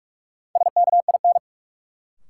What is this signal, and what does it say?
Morse code, heard as a keyed tone, about 46 wpm, 700 Hz. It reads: SKIN
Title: Sentences from Top 1300 Words - New Word Drill - 45wpm